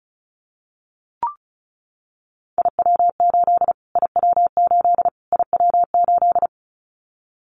Morse code, heard as a keyed tone, about 34 wpm, 700 Hz. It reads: IW8 IW8 IW8
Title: Callsign Prefixes for Top 25 DXCC - Repeated 3x - 35wpm